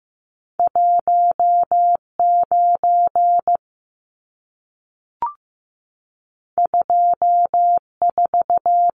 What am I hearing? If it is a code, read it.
Morse code, heard as a keyed tone, about 15 wpm, 700 Hz. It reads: 19 24
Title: Numbers - 2-Digit 15wpm